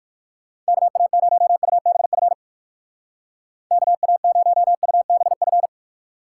Morse code, heard as a keyed tone, about 55 wpm, 700 Hz. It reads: KA0UBF KA0UBF
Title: Speed Racing - USA 2x3 Call Signs 45wpm